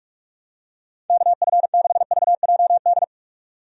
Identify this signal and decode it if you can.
Morse code, heard as a keyed tone, about 48 wpm, 700 Hz. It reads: KF6VJD
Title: USA Call Signs - 6 Characters - 45wpm